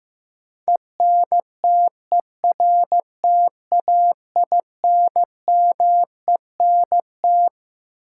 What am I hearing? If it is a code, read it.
Morse code, heard as a keyed tone, about 15 wpm, 700 Hz. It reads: ENTERTAINMENT
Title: Sentences from Top 1000 Words - Review Part 6 - New Word Drill - 15wpm